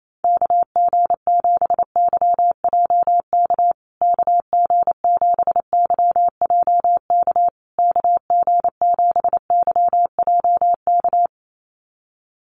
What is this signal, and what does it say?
Morse code, heard as a keyed tone, about 28 wpm, 700 Hz. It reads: KG7YJK KG7YJK KG7YJK
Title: USA Call Signs Repeated 3x 28wpm